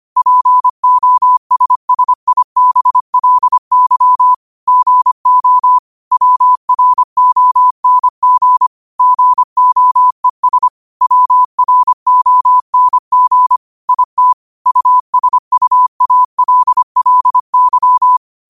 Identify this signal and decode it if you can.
Morse code, heard as a keyed tone, about 25 wpm, 1000 Hz. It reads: POSSIBLY GO WRONG GOES WRONG IT USUALLY
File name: qotdfiles / 25wpm